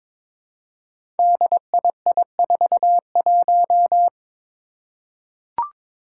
Morse code, heard as a keyed tone, about 23 wpm, 700 Hz. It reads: DII41
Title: Five Challenging Letter-Number Combinations 22wpm